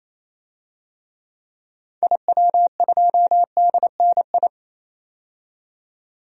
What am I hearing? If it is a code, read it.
Morse code, heard as a keyed tone, about 28 wpm, 700 Hz. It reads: IW2DNI
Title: DXCC-100 Call Signs - 6 Characters - 28wpm